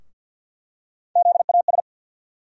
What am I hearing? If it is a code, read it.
Morse code, heard as a keyed tone, about 52 wpm, 700 Hz. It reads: ZAS
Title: USA Call Sign Postfixes - 3 Letters 50wpm